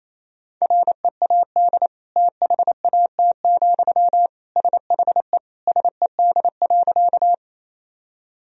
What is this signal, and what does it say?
Morse code, heard as a keyed tone, about 28 wpm, 700 Hz. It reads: READ THAT, SHE SED.
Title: Sentences from Top 200 Words and Abbreviations 28wpm